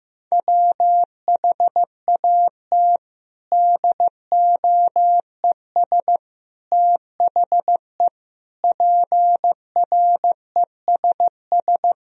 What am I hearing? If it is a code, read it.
Morse code, heard as a keyed tone, about 16 wpm, 700 Hz. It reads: WHAT DOES THE PRESS
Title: Sentences from Top 1000 Words - Review Part 3 - 15wpm